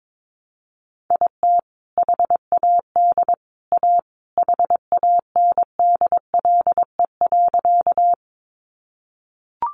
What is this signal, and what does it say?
Morse code, heard as a keyed tone, about 22 wpm, 700 Hz. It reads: IT HAD A HANDLE.
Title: Sentences from Top 1100 Words 22wpm